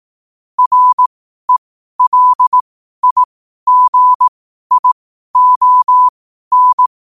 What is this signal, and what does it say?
Morse code, heard as a keyed tone, about 18 wpm, spaced as for 12 wpm, 1000 Hz. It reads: RELIGION
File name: 12wpm